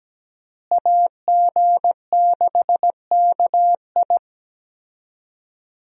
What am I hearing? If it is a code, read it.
Morse code, heard as a keyed tone, about 17 wpm, 700 Hz. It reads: AG6KI
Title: USA Call Signs - 5 Characters - Repeated 3x - 17wpm